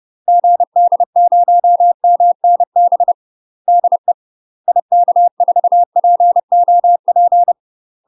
Morse code, heard as a keyed tone, about 30 wpm, 700 Hz. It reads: GD0MNB DE IK4POP
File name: UK MASTER QSOs TRAINING / 30WPM / UK MASTER QSOs 0043